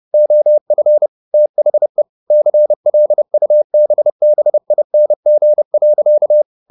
Morse code, heard as a keyed tone, about 30 wpm, 600 Hz. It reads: OF THE CLUBBING.
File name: the old man and the sea / the old man and the sea 1 30wpm